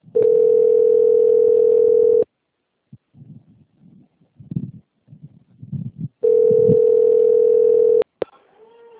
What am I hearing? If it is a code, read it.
Ringback tone (North America), 440 + 480 Hz, with speech.
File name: out- - - - - .42176